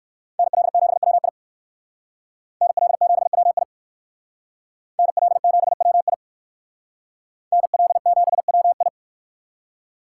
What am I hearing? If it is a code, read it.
Morse code, heard as a keyed tone, about 57 wpm, 700 Hz. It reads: NL7WI NL7WI NL7WI NL7WI